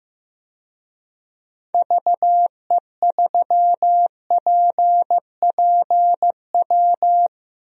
Morse code, heard as a keyed tone, about 15 wpm, 700 Hz. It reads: VE3PPW
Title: DXCC-100 Call Signs - 6 Characters - 15wpm